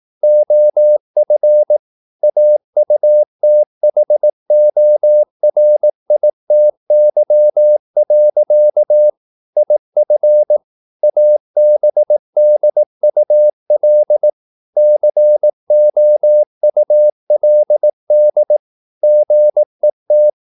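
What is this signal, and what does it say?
Morse code, heard as a keyed tone, about 18 wpm, 600 Hz. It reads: OF AUTHORITY. IF ABDUL COULD GET
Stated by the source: Morse practice at 18 wpm, English text, from CW med Gnister.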